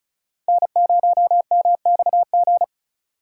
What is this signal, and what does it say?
Morse code, heard as a keyed tone, about 34 wpm, 700 Hz. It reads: N0MXG
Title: USA Call Signs - 3x Character Spacing - 35wpm